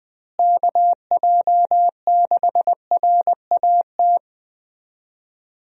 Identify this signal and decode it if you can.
Morse code, heard as a keyed tone, about 20 wpm, 700 Hz. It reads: KJ6RAT
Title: USA Call Signs - 6 Characters - Repeated 3x - 20wpm